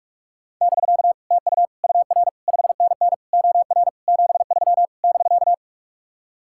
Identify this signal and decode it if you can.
Morse code, heard as a keyed tone, about 60 wpm, spaced as for 52 wpm, 700 Hz. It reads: <BK> TU UR 5NN OR 73 <BK>
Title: Speed-Racing Common QSO Phrases - 45wpm